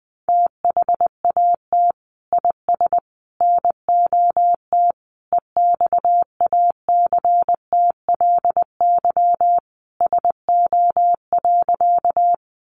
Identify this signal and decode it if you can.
Morse code, heard as a keyed tone, about 20 wpm, 700 Hz. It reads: THAT IS NOT EXACTLY SO.
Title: Sentences from Top 900 Words 20wpm